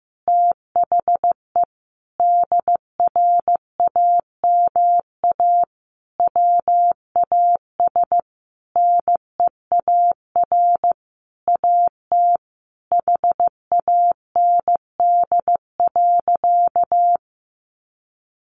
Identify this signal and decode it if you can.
Morse code, heard as a keyed tone, about 15 wpm, 700 Hz. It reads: THE DRAMA WAS NEAR AT HAND.
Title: Sentences from Top 1500 Words 15wpm